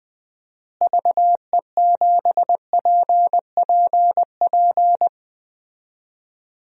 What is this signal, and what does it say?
Morse code, heard as a keyed tone, about 21 wpm, 700 Hz. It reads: VE7PPP
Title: DXCC-100 Call Signs - 6 Characters - Repeated 3x - 20wpm